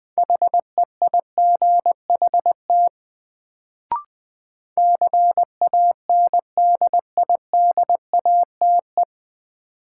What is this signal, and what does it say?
Morse code, heard as a keyed tone, about 20 wpm, 700 Hz. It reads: HEIGHT CANDIDATE
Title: Sentences from Top 1600 Words - New Word Drill Alt - 20wpm